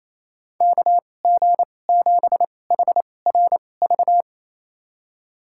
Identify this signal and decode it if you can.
Morse code, heard as a keyed tone, about 28 wpm, spaced as for 20 wpm, 700 Hz. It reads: KG7HRV
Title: USA Call Signs - 6 Characters - 2x Character Spacing - 28wpm